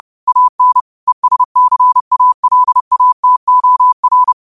Morse code, heard as a keyed tone, about 30 wpm, 1000 Hz. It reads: AN ESCALATOR
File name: qotdfiles / 30wpm